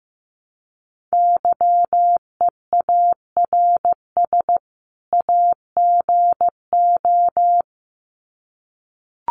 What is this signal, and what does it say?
Morse code, heard as a keyed tone, about 15 wpm, 700 Hz. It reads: YEARS AGO
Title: Sets of 2 Words 15wpm